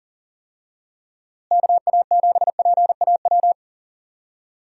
Morse code, heard as a keyed tone, about 40 wpm, 700 Hz. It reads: KA7PAW
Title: USA Call Signs - 6 Characters - Repeated 3x - 40wpm